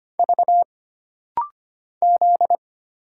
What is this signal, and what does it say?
Morse code, heard as a keyed tone, about 25 wpm, 700 Hz. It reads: V Z
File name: Single Character (Letter:Number:Punctuation:Prosign) - Rapid-Fire - 25wpm